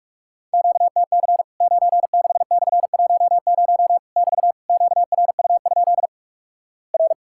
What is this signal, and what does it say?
Morse code, heard as a keyed tone, about 48 wpm, 700 Hz. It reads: QTC 96/10 <BT> QRU? R
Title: Worked All Europe DX Contest - QTC Exchange - 45wpm